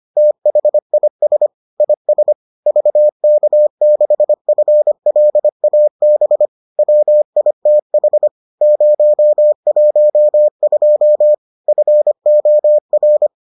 Morse code, heard as a keyed tone, about 25 wpm, 600 Hz. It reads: THIS IS VK6FLAB WITH 012 FOR